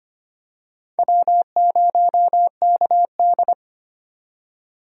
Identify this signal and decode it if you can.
Morse code, heard as a keyed tone, about 25 wpm, 700 Hz. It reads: W0KD
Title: USA Call Signs - 4 Characters - 25wpm